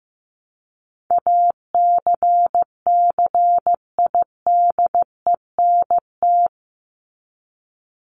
Morse code, heard as a keyed tone, about 15 wpm, 700 Hz. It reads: ACCIDENT
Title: Top 1751-2000 Words 15wpm